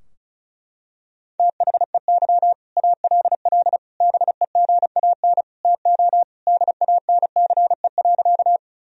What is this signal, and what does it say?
Morse code, heard as a keyed tone, about 34 wpm, 700 Hz. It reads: THEY ALL BEGAN TO DANCE.
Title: Sentences from Top 1000 Words - Review Part 4 - 35wpm